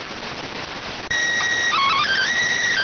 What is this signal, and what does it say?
SSTV (slow-scan television), PD 290.